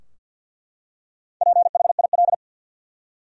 Morse code, heard as a keyed tone, about 49 wpm, 700 Hz. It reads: PHIL
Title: USA Common Names - Max 5 Letters - 50wpm